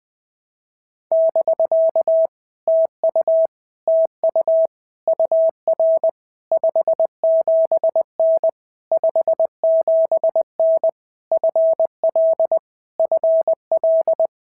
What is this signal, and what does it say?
Morse code, heard as a keyed tone, about 19 wpm, 650 Hz. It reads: <BK> TU TU UR 57N 57N FL FL
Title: POTA QSO Practice - 20wpm